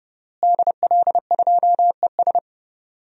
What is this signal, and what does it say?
Morse code, heard as a keyed tone, about 32 wpm, 700 Hz. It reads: DL2ES
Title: DXCC-100 Call Signs - 5 Characters - Repeated 2x - 30wpm